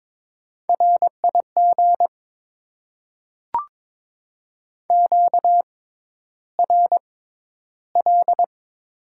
Morse code, heard as a keyed tone, about 22 wpm, spaced as for 11 wpm, 700 Hz. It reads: RIG QRL
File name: Top 20 QSO Elements - 6x Character Spacing - 22wpm